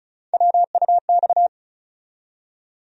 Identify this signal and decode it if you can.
Morse code, heard as a keyed tone, about 34 wpm, 700 Hz. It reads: WUX
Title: USA Call Sign Postfixes - 3 Letters 35wpm